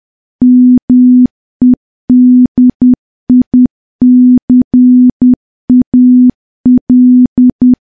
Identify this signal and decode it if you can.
Morse code, heard as a keyed tone, about 10 wpm, 250 Hz. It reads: MEDICAL
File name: morse / morse-training-10WPM